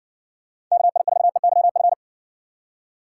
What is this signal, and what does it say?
Morse code, heard as a keyed tone, about 60 wpm, 700 Hz. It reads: KI4EYF